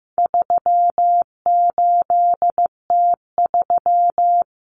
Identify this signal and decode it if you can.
Morse code, heard as a keyed tone, about 15 wpm, 700 Hz. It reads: 38T3
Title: Common Cut Numbers - 4-Digit 15wpm